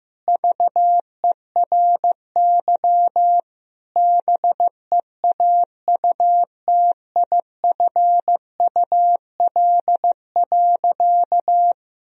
Morse code, heard as a keyed tone, about 16 wpm, 700 Hz. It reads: VERY BEAUTIFUL.
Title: Sentences from Top 1000 Words - Review Part 1 - 15wpm